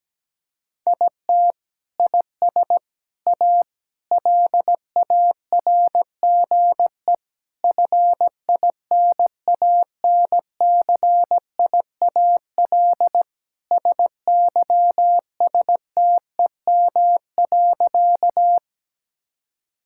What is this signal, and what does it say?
Morse code, heard as a keyed tone, about 17 wpm, 700 Hz. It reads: IT IS A LARGE FINANCIAL SYSTEM.
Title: Sentences from Top 1000 Words - Review Part 5 - 17wpm